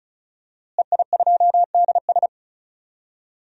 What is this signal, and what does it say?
Morse code, heard as a keyed tone, about 36 wpm, 700 Hz. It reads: EI2DS
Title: DXCC-100 Call Signs - 5 Characters - Repeated 3x - 35wpm